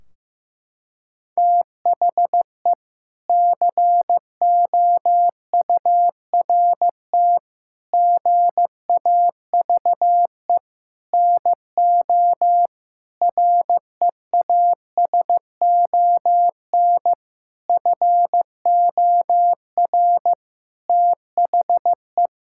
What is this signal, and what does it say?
Morse code, heard as a keyed tone, about 16 wpm, 700 Hz. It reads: THE COURT GAVE NO REASON FOR THE
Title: Sentences from Top 1600 Words 15wpm